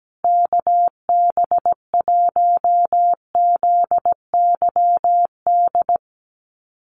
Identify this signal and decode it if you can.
Morse code, heard as a keyed tone, about 17 wpm, 700 Hz. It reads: KB1ZYD
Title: USA Call Signs Repeated 3x 17wpm